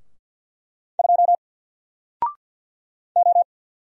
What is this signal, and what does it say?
Morse code, heard as a keyed tone, about 50 wpm, 700 Hz. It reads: 2 O
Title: Single Letter-Number 50wpm